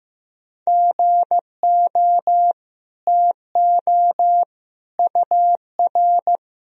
Morse code, heard as a keyed tone, about 15 wpm, 700 Hz. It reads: GO TO UR
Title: Sentences from Top 300 Words and Abbreviations 15wpm